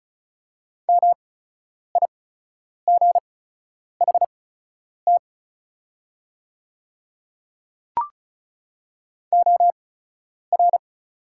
Morse code, heard as a keyed tone, about 34 wpm, spaced as for 10 wpm, 700 Hz. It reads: MIGHT OR
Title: 8x Character Spacing - Top 300 Words - 35wpm